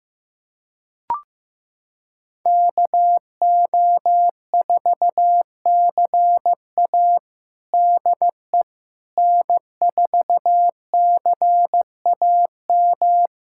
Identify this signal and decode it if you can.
Morse code, heard as a keyed tone, about 15 wpm, 700 Hz. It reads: KO4CA DE N4CAM
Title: USA Callsign DE Callsign - 5 Characters - 15wpm